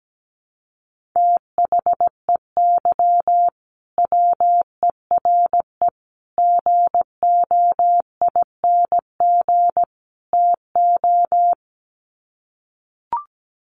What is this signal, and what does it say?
Morse code, heard as a keyed tone, about 17 wpm, 700 Hz. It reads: THEY WERE GOING TO
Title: Sets of 4 Words 17wpm